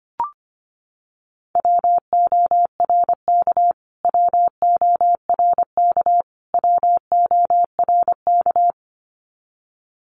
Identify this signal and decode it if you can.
Morse code, heard as a keyed tone, about 24 wpm, 700 Hz. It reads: WORK WORK WORK
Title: Top 200 Words Repeated 25wpm